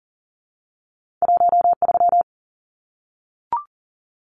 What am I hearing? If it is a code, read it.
Morse code, heard as a keyed tone, about 40 wpm, 700 Hz. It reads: 13
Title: Numbers - 2-Digit 40wpm